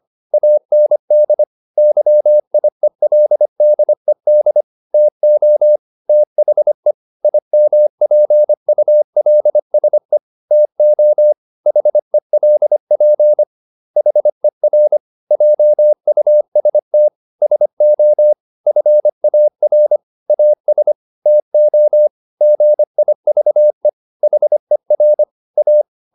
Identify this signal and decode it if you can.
Morse code, heard as a keyed tone, about 25 wpm, 600 Hz. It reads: AND YIELDED TO THE IMPULSE TO HELP HER JUST SO FAR AS TO GIVE HER A